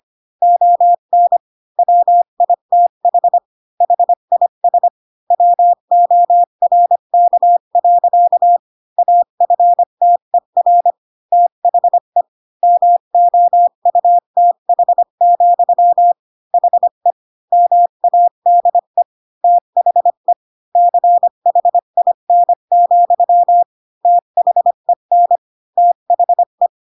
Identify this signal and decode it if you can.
Morse code, heard as a keyed tone, about 25 wpm, 700 Hz. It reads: ON WITH HIS WORK. AFTER THE MOUTH, HE MADE THE CHIN, THEN THE
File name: Pinocchio / Pinocchio ch3 20 25 ik1hge